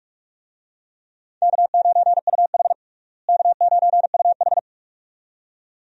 Morse code, heard as a keyed tone, about 42 wpm, 700 Hz. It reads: K9UH K9UH
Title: USA Call Signs - 4 Characters - Repeated 2x - 45wpm